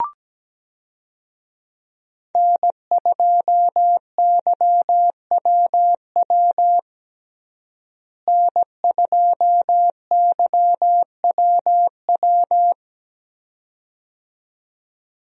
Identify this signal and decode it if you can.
Morse code, heard as a keyed tone, about 17 wpm, 700 Hz. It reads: N2YWW N2YWW
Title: USA Call Signs - 5 Characters - Repeated 2x - 17wpm